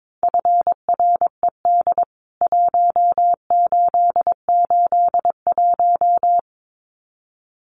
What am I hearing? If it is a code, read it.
Morse code, heard as a keyed tone, about 22 wpm, 700 Hz. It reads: FRED 1881
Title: CWT Exchange 22wpm